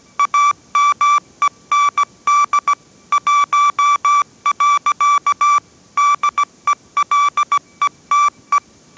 Morse code, heard as a keyed tone, about 17 wpm, 1160 Hz. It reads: AMEND 1. DELETE
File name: 18wpm